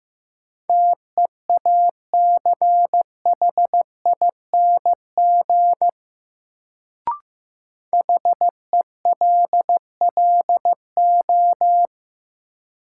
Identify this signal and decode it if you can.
Morse code, heard as a keyed tone, about 15 wpm, 700 Hz. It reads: TEACHING HELLO
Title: Sentences from Top 1100 Words - New Word Drill Alt - 15wpm